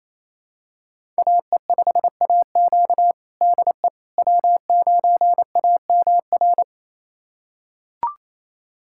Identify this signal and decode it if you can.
Morse code, heard as a keyed tone, about 28 wpm, 700 Hz. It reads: AE5AQ DE W9AMR
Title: USA Callsign DE Callsign - 5 Characters - 28wpm